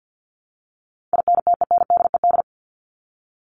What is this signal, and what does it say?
Morse code, heard as a keyed tone, about 50 wpm, 700 Hz. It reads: INTENDED
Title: ED Suffix - Top 100 Words with Base Words 50wpm